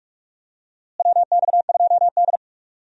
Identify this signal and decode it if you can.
Morse code, heard as a keyed tone, about 46 wpm, 700 Hz. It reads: WX2D
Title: USA Call Signs - 4 Characters - Repeated 2x - 45wpm